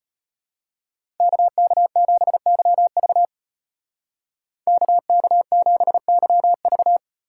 Morse code, heard as a keyed tone, about 34 wpm, 700 Hz. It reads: KK7YV KK7YV